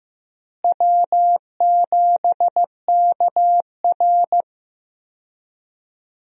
Morse code, heard as a keyed tone, about 15 wpm, 700 Hz. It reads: W7KR
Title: USA Call Signs - 4 Characters - Repeated 2x - 15wpm